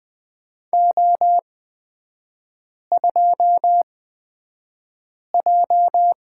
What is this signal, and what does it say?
Morse code, heard as a keyed tone, about 20 wpm, spaced as for 5 wpm, 700 Hz. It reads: O2J
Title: 243 Lesson 39e - Letter z - Three Characters 20wpm